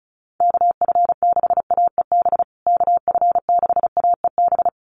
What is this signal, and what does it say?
Morse code, heard as a keyed tone, about 34 wpm, 700 Hz. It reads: KF6AEB KF6AEB
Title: USA Call Signs Repeated 3x 35wpm